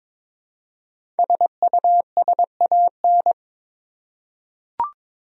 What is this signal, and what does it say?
Morse code, heard as a keyed tone, about 22 wpm, 700 Hz. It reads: SUSAN
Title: USA Common Names - Max 5 Letters - 22wpm